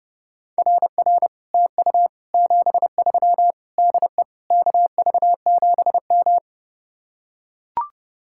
Morse code, heard as a keyed tone, about 30 wpm, 700 Hz. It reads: RR TU 73 DE KV7M